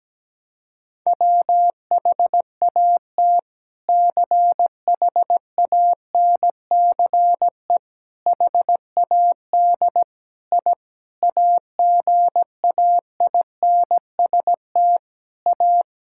Morse code, heard as a keyed tone, about 17 wpm, 700 Hz. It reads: WHAT CHANCE HAD I AGAINST A
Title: Sentences from Top 1000 Words - Review Part 6 - 17wpm